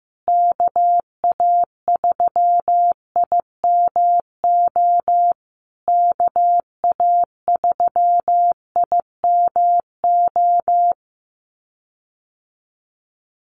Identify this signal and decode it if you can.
Morse code, heard as a keyed tone, about 14 wpm, 700 Hz. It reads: KA3IMO KA3IMO
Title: USA Call Signs Repeated 3x 15wpm